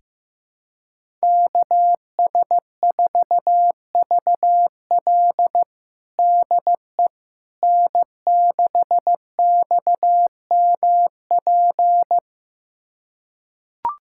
Morse code, heard as a keyed tone, about 15 wpm, 700 Hz. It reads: KS4VL DE N6XMP
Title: USA Callsign DE Callsign - 5 Characters - 15wpm